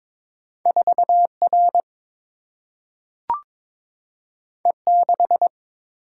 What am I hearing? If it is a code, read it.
Morse code, heard as a keyed tone, about 22 wpm, 700 Hz. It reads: 4 R E 6
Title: Single-Letter Single-Number 22wpm